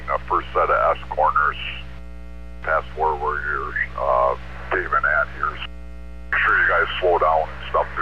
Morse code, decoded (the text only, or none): none